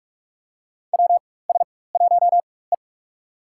Morse code, heard as a keyed tone, about 43 wpm, spaced as for 21 wpm, 700 Hz. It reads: WS1E
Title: USA Call Signs - 4x Character Spacing - 45wpm